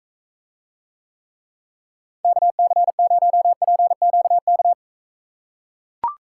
Morse code, heard as a keyed tone, about 41 wpm, 700 Hz. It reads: KC0PQK